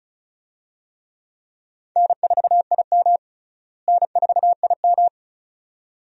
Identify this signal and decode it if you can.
Morse code, heard as a keyed tone, about 36 wpm, 700 Hz. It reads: N4IM N4IM
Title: USA Call Signs - 4 Characters - Repeated 2x - 35wpm